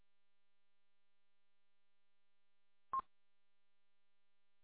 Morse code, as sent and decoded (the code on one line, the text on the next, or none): none